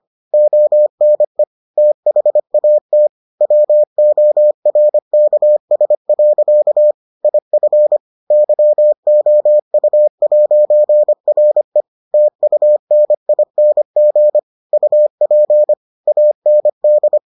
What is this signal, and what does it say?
Morse code, heard as a keyed tone, about 25 wpm, 600 Hz. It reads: ONE THAT WORKS. IF YOU'RE TUNING UP AND